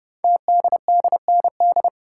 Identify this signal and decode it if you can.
Morse code, heard as a keyed tone, about 30 wpm, 700 Hz. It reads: TDDND
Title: Code Groups - 5 Letters - 30wpm